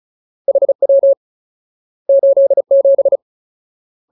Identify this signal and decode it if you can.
Morse code, heard as a keyed tone, about 34 wpm, 550 Hz. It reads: HW 87
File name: Ultimate Word List / 35WPM Ultimate